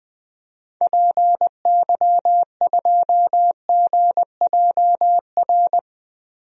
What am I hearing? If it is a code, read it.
Morse code, heard as a keyed tone, about 20 wpm, 700 Hz. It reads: PY2GJR
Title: DXCC-100 Call Signs - 6 Characters - 20wpm